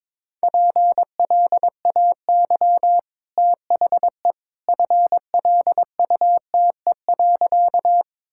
Morse code, heard as a keyed tone, about 22 wpm, 690 Hz. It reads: PLAY THE FLUTE.
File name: practice / 22wpm / Easy Sentences with Hints - 22wpm